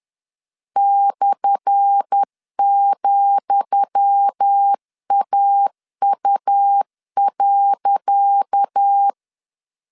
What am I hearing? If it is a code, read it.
Morse code, heard as a keyed tone, about 10 wpm, 800 Hz. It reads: /,AU.
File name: LEZIONI / 10 SIGNS / 10D COMA 10wpm 3013